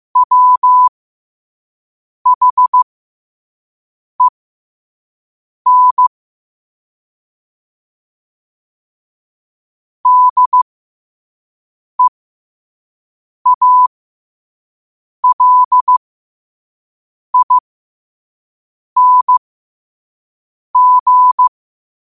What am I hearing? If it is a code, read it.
Morse code, heard as a keyed tone, about 15 wpm, spaced as for 5 wpm, 1000 Hz. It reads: WHEN DEALING
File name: qotdfiles / 5wpm